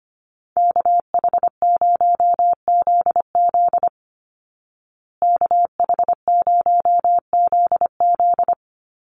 Morse code, heard as a keyed tone, about 25 wpm, 700 Hz. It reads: KH0ZZ KH0ZZ
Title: Call Signs Repeated 2x - 25wpm